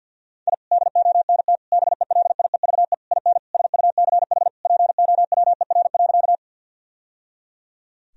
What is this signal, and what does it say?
Morse code, heard as a keyed tone, about 52 wpm, 700 Hz. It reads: I DONT BELIEVE IN SUCH POWER.
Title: Sentences from Top 1000 Words - Review Part 2 - 50wpm